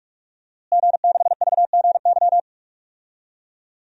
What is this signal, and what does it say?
Morse code, heard as a keyed tone, about 46 wpm, 700 Hz. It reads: G6VGY
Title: DXCC-100 Call Signs - 5 Characters - Repeated 2x - 45wpm